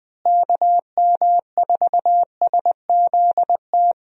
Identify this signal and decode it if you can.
Morse code, heard as a keyed tone, about 20 wpm, 700 Hz. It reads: KM4SZT
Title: USA Call Signs - 6 Characters - Repeated 3x - 20wpm